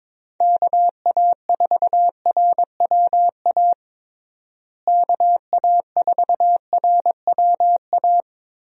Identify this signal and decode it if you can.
Morse code, heard as a keyed tone, about 21 wpm, 700 Hz. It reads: KA4RWA KA4RWA
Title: USA Call Signs - 6 Characters - Repeated 2x - 22wpm